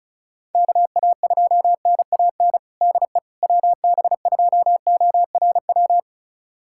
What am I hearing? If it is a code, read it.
Morse code, heard as a keyed tone, about 36 wpm, 700 Hz. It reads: KA2NAN DE WB2ORW
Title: USA Callsign DE Callsign 35wpm